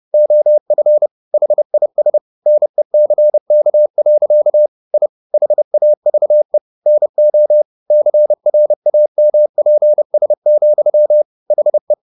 Morse code, heard as a keyed tone, about 30 wpm, 600 Hz. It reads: OF HIS NECK. I HAVE NO CRAMPS, HE